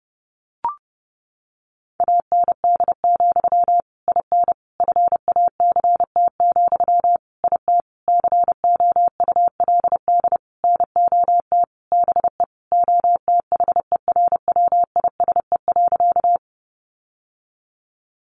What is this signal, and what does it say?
Morse code, heard as a keyed tone, about 30 wpm, 700 Hz. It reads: AND, IN FACT, IT COULD NOT BE OTHERWISE.
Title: Sentences from Top 1200 Words 30wpm